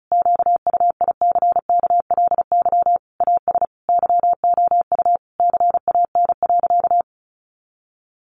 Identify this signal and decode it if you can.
Morse code, heard as a keyed tone, about 34 wpm, 700 Hz. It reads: QUICKLY AS YOU CAN.
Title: Sentences from Top 1100 Words 35wpm